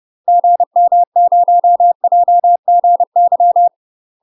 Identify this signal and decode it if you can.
Morse code, heard as a keyed tone, about 30 wpm, 700 Hz. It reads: GM0JGY
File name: UK MASTER QSOs TRAINING / 30WPM / UK MASTER QSOs 0040